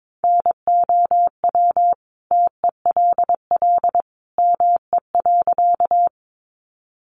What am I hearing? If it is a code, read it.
Morse code, heard as a keyed tone, about 22 wpm, 700 Hz. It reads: NOW TELL ME.
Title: Sentences from Top 200 Words 22wpm